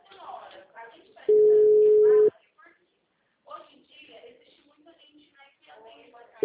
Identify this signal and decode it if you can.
Ringback tone (Europe, 425 Hz), with speech.
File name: out- -2016- - - .62233